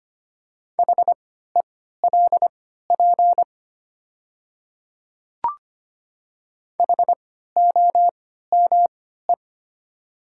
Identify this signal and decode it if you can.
Morse code, heard as a keyed tone, about 25 wpm, spaced as for 14 wpm, 690 Hz. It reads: HELP HOME
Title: 3x Character Spacing - Top 300 Words - 25wpm